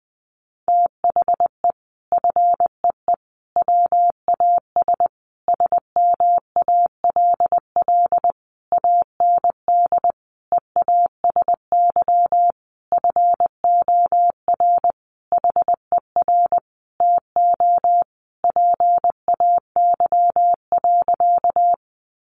Morse code, heard as a keyed tone, about 20 wpm, 700 Hz. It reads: THE FEE WAS SMALL AND EASY FOR HER TO PAY.
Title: Sentences from Top 1500 Words 20wpm